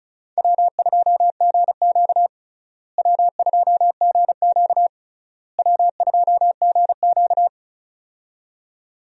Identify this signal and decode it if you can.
Morse code, heard as a keyed tone, about 34 wpm, 700 Hz. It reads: W2GQ W2GQ W2GQ
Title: USA Call Signs - 4 Characters - Repeated 3x - 35wpm